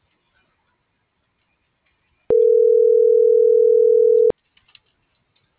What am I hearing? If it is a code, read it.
Ringback tone (North America), 440 + 480 Hz.
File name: CENTRAL GOMSA / out lizeth.morales